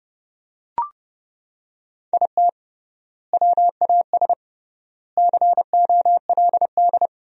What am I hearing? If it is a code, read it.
Morse code, heard as a keyed tone, about 29 wpm, 700 Hz. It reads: IT WAS COLD
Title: 3x Word-Spacing - Sentences from Top 500 Words - 30wpm